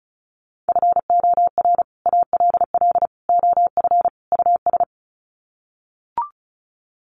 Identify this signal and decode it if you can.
Morse code, heard as a keyed tone, about 34 wpm, 700 Hz. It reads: FOR ALL OF US
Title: Sets of 4 Words 35wpm